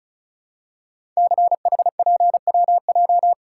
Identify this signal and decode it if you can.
Morse code, heard as a keyed tone, about 36 wpm, 690 Hz. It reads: CHPWJ
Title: Code Groups - 5 Letters - 35wpm